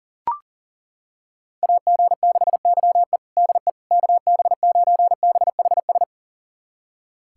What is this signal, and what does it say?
Morse code, heard as a keyed tone, about 40 wpm, 700 Hz. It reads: AG6YE DE KB9BHS
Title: USA Callsign DE Callsign 40wpm